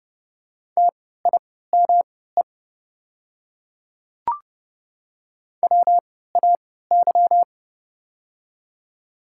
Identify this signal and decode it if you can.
Morse code, heard as a keyed tone, about 31 wpm, spaced as for 17 wpm, 700 Hz. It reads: TIME WAY
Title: 3x Character Spacing - Top 100 Words - 30wpm